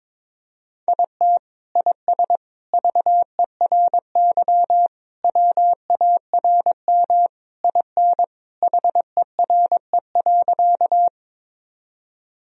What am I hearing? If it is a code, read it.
Morse code, heard as a keyed tone, about 22 wpm, 700 Hz. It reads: IT IS VERY WARM IN HERE.
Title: Sentences from Top 1000 Words - Review Part 2 - 22wpm